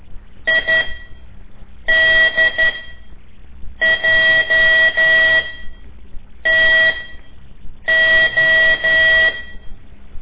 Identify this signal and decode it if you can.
Morse code, heard as a keyed tone, about 12 wpm, spaced as for 6 wpm, 1950 Hz. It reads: IDJTO
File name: ik1weg2 / CORSI CW / lez5-1